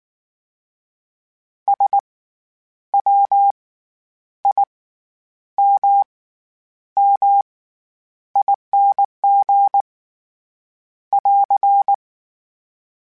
Morse code, heard as a keyed tone, about 19 wpm, spaced as for 9 wpm, 800 Hz. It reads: SWIMMING <AR>
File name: Special INGs / Special INGs-mp3s / SlowToFaster ing @18